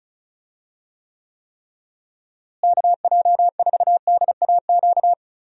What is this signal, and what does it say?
Morse code, heard as a keyed tone, about 36 wpm, 700 Hz. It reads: KJ4DAQ
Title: USA Call Signs - 6 Characters - Repeated 3x - 35wpm